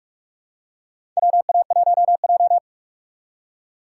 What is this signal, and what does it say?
Morse code, heard as a keyed tone, about 46 wpm, 700 Hz. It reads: WA1J
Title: USA Call Signs - 4 Characters - 3x Character Spacing - 45wpm